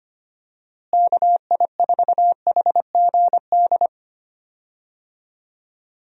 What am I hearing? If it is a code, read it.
Morse code, heard as a keyed tone, about 25 wpm, 700 Hz. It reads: KI4HGD
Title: USA Call Signs - 6 Characters - Repeated 2x - 25wpm